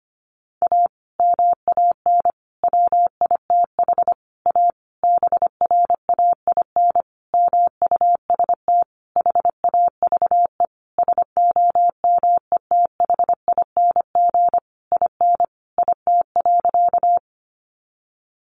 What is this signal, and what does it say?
Morse code, heard as a keyed tone, about 25 wpm, 700 Hz. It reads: A MAN WITH A BRAIN MUST HAVE SOMETHING IN IT.
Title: Sentences from Top 1100 Words 25wpm